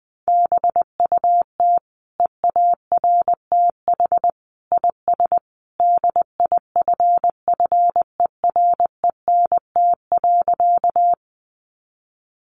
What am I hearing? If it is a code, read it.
Morse code, heard as a keyed tone, about 20 wpm, 700 Hz. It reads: BUT EARTH IS DIFFERENT.
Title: Sentences from Top 600 Words 20wpm